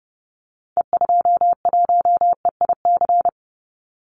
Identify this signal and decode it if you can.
Morse code, heard as a keyed tone, about 30 wpm, 700 Hz. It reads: E21EIC
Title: Call Signs Repeated 3x - 30wpm